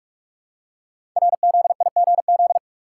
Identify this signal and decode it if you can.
Morse code, heard as a keyed tone, about 44 wpm, 690 Hz. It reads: RZIGZ
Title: Code Groups - 5 Letters - 45wpm